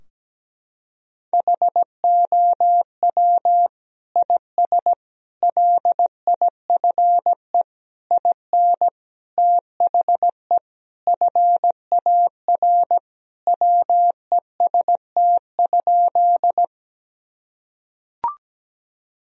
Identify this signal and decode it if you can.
Morse code, heard as a keyed tone, about 17 wpm, 700 Hz. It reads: HOW IS LIFE IN THE FAR WEST?
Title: Sentences from Top 1000 Words - Review Part 2 - 17wpm